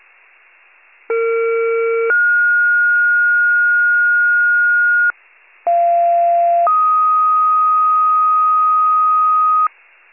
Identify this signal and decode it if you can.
Two-tone page, heard as a one-second tone, then a three-second tone, about 460 then 1430 Hz, about 690 then 1250 Hz. It is heard from 1.1 to 9.7 s.
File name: pager / 41 FIRE / wcalert